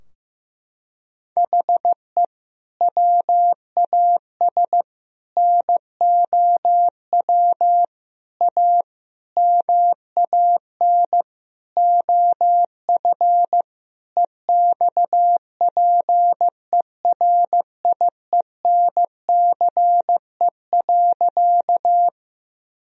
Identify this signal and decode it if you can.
Morse code, heard as a keyed tone, about 15 wpm, 700 Hz. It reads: HE WAS NOW A MAN OF EXPERIENCE.
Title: Sentences from Top 1000 Words - Review Part 3 - 15wpm